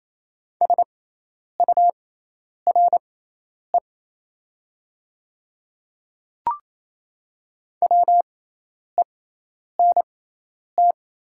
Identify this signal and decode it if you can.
Morse code, heard as a keyed tone, about 28 wpm, spaced as for 10 wpm, 700 Hz. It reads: SURE WENT
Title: 6x Character Spacing - Top 300 Words - 28wpm